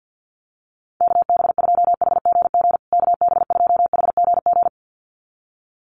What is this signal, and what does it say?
Morse code, heard as a keyed tone, about 50 wpm, 700 Hz. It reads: KB2HGG KB2HGG
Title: USA Call Signs Repeated 2x 50wpm